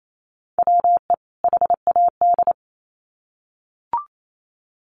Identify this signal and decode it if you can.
Morse code, heard as a keyed tone, about 28 wpm, 700 Hz. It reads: WE HAD
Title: Sets of 2 Words 28wpm